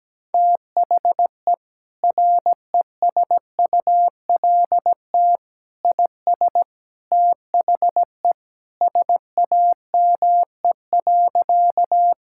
Morse code, heard as a keyed tone, about 17 wpm, 700 Hz. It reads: THE RESULT IS THE SAME.
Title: Sentences from Top 1000 Words - Review Part 5 - 17wpm